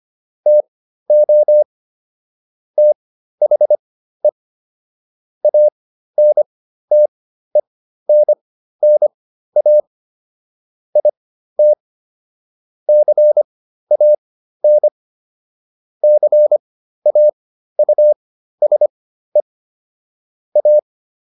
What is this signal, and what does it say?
Morse code, heard as a keyed tone, about 25 wpm, spaced as for 13 wpm, 600 Hz. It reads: TO THE ANTENNA IT CAN CAUSE A